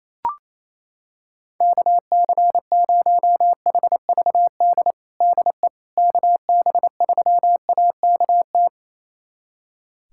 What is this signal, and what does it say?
Morse code, heard as a keyed tone, about 28 wpm, 700 Hz. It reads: KC0HVD DE KB3AKT
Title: USA Callsign DE Callsign 28wpm